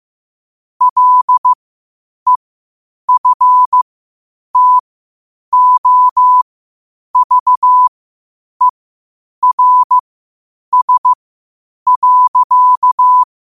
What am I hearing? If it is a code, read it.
Morse code, heard as a keyed tone, about 15 wpm, spaced as for 8 wpm, 1000 Hz. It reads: LEFTOVERS.
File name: qotdfiles / 8wpm